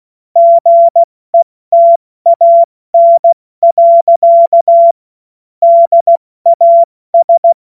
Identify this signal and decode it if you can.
Morse code, heard as a keyed tone, about 16 wpm, 680 Hz. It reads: GETAN. DAS
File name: Morsekurs / Witze / Witz 28.November 2023 70 F80